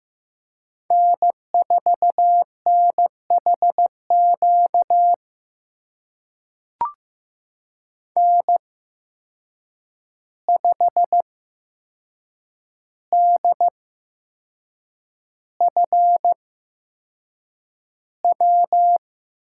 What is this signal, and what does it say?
Morse code, heard as a keyed tone, about 15 wpm, spaced as for 6 wpm, 700 Hz. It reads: N4NHQ N5DFW
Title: USA Call Signs - 8x Character Spacing - 15wpm